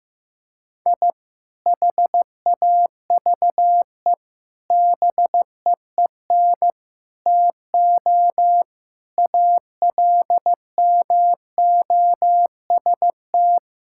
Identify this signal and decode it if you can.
Morse code, heard as a keyed tone, about 15 wpm, 700 Hz. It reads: I HAVE BEEN TO ALMOST
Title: Sentences from Top 1000 Words - Review Part 5 - 15wpm